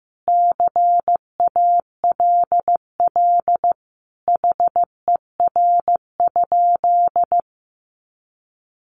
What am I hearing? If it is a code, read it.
Morse code, heard as a keyed tone, about 15 wpm, 700 Hz. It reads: CALL HER?
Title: Sentences from Top 100 Words 15wpm